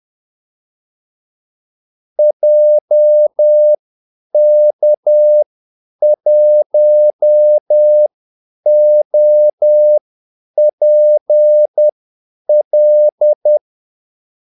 Morse code, heard as a keyed tone, about 10 wpm, spaced as for 8 wpm, 600 Hz. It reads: JK1OPL
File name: Callsigns-10-8wpm